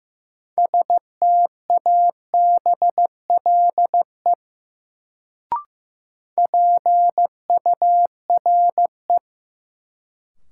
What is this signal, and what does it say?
Morse code, heard as a keyed tone, about 15 wpm, 700 Hz. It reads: STABLE PURE
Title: Sentences from Top 1500 Words - New Word Drill Alt - 15wpm